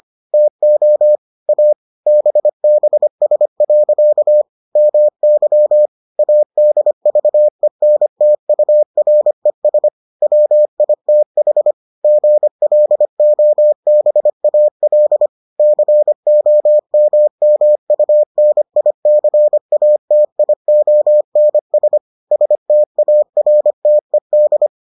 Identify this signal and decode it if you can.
Morse code, heard as a keyed tone, about 25 wpm, 600 Hz. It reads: TO A BBS. MY ADVENTURES WITH GLOBAL COMMUNICATIONS STARTED